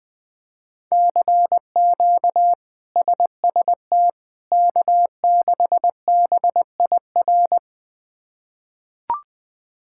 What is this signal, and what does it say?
Morse code, heard as a keyed tone, about 21 wpm, 700 Hz. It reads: CQ SST K6BIR
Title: K1USN SST QSO Practice - 20wpm